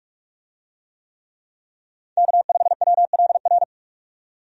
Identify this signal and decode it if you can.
Morse code, heard as a keyed tone, about 46 wpm, 700 Hz. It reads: K5WLR
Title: USA Call Signs - 5 Characters - 45wpm